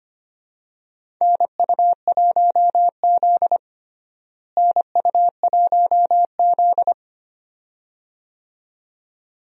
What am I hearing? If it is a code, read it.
Morse code, heard as a keyed tone, about 25 wpm, 700 Hz. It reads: NU1Z NU1Z
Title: USA Call Signs - 4 Characters - Repeated 2x - 25wpm